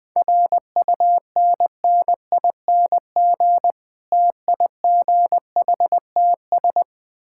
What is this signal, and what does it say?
Morse code, heard as a keyed tone, about 20 wpm, 700 Hz. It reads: RUNNING TIGHTS
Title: Difficult Words with Hints - 20wpm